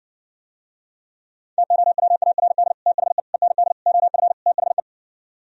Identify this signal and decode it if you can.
Morse code, heard as a keyed tone, about 69 wpm, 700 Hz. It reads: TOWARD THE END OF THE